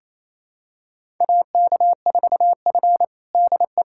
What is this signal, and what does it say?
Morse code, heard as a keyed tone, about 28 wpm, 700 Hz. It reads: AK4F DE
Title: USA Callsign DE Callsign - 4 Characters - 28wpm